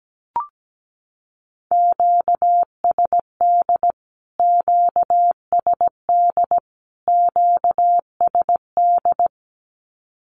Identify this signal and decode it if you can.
Morse code, heard as a keyed tone, about 17 wpm, 700 Hz. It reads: QSD QSD QSD
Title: Q-Codes Repeated 3x 17wpm